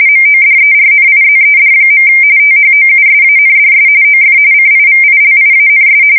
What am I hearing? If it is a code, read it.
RTTY, about 45.45 baud. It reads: MODERNIZED. THOUSANDS OF CIVILIANS
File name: fsk170 / USS-New-Jersey